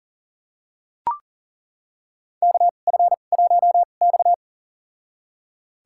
Morse code, heard as a keyed tone, about 40 wpm, spaced as for 29 wpm, 700 Hz. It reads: KF1X
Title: USA Call Signs - 2x Character Spacing - 40wpm